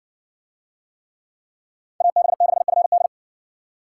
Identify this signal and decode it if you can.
Morse code, heard as a keyed tone, about 54 wpm, 700 Hz. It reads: AB6FD